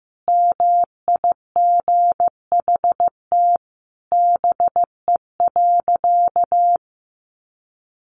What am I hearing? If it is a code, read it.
Morse code, heard as a keyed tone, about 15 wpm, 700 Hz. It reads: MIGHT BE.
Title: Sentences from Top 1500 Words 15wpm